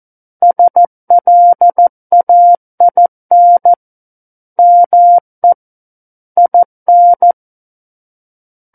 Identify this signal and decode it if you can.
Morse code, heard as a keyed tone, about 14 wpm, 700 Hz. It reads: SLAIN ME IN
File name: Book One / 05-Chapter 5 - 14 WPM